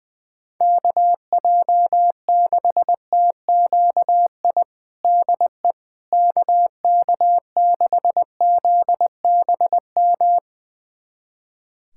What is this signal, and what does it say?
Morse code, heard as a keyed tone, about 20 wpm, 690 Hz. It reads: KJ6TQI DE KK6ZBM
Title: USA Callsign DE Callsign 20wpm